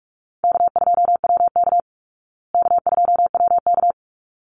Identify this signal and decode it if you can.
Morse code, heard as a keyed tone, about 44 wpm, 700 Hz. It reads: K2WK K2WK
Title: Call Signs Repeated 3x - 45wpm